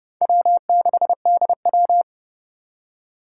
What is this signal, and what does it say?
Morse code, heard as a keyed tone, about 27 wpm, 700 Hz. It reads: W6DW